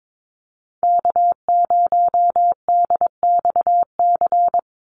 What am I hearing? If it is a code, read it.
Morse code, heard as a keyed tone, about 22 wpm, 700 Hz. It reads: K0DXC
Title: Call Signs Repeated 3x - 22wpm